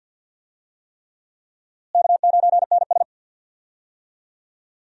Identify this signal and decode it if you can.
Morse code, heard as a keyed tone, about 48 wpm, 700 Hz. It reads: K9NS
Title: USA Call Signs - 4 Characters - Repeated 3x - 50wpm